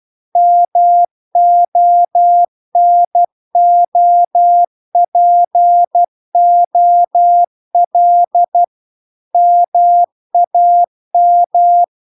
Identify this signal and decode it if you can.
Morse code, heard as a keyed tone, about 12 wpm, 700 Hz. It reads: MONOPOL MAM